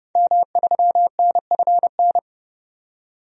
Morse code, heard as a keyed tone, about 31 wpm, 700 Hz. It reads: M3NFN
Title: DXCC-100 Call Signs - 5 Characters - 30wpm